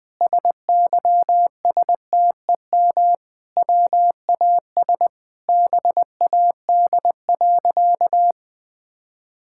Morse code, heard as a keyed tone, about 21 wpm, 700 Hz. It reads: SYSTEM WAS BAD.
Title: Sentences from Top 1000 Words - Review Part 5 - 20wpm